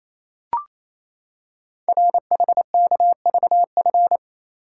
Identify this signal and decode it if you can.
Morse code, heard as a keyed tone, about 27 wpm, 700 Hz. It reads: RHKVF
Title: Code Groups - 5 Letters - 28wpm